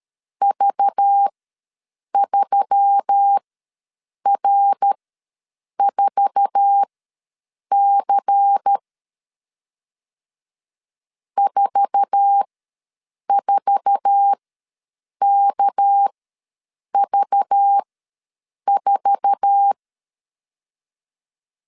Morse code, heard as a keyed tone, about 12 wpm, spaced as for 7 wpm, 800 Hz. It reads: V3R4C 44KV4
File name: LEZIONI / 8 YJ45 / 8 YJ4 ws 12wpm 3249